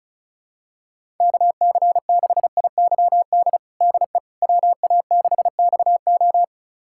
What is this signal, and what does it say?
Morse code, heard as a keyed tone, about 36 wpm, 700 Hz. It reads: KC6IYD DE WA6XO
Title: USA Callsign DE Callsign 35wpm